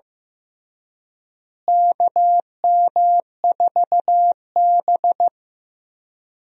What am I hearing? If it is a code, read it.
Morse code, heard as a keyed tone, about 15 wpm, 700 Hz. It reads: KM4B
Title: USA Call Signs - 4 Characters - Repeated 3x - 15wpm